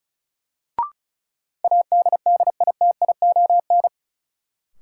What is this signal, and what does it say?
Morse code, heard as a keyed tone, about 36 wpm, 700 Hz. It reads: ADDITION
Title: Sentences from Top 1000 Words - Review Part 6 - New Word Drill - 35wpm